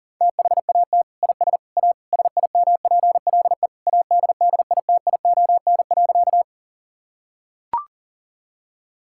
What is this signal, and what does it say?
Morse code, heard as a keyed tone, about 40 wpm, 700 Hz. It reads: THAT IS A SIMPLE ADDITION.
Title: Sentences from Top 1000 Words - Review Part 6 - 40wpm